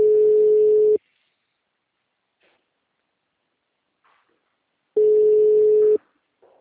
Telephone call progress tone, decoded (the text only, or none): ringback tone